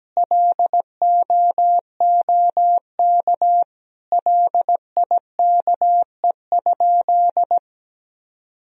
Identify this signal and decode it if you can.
Morse code, heard as a keyed tone, about 17 wpm, 700 Hz. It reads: LOOK LIKE?
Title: Sentences from Top 100 Words and Abbreviations 17wpm